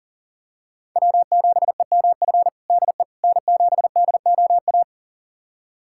Morse code, heard as a keyed tone, about 40 wpm, 700 Hz. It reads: W7EMF DE N7DOA
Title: USA Callsign DE Callsign - 5 Characters - 40wpm